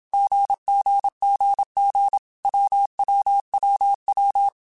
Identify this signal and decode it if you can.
Morse code, heard as a keyed tone, about 27 wpm, 800 Hz. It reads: GGGG WWWW
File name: Basic Sets / Basic Letter Groups Collective7 / AWUNDG@25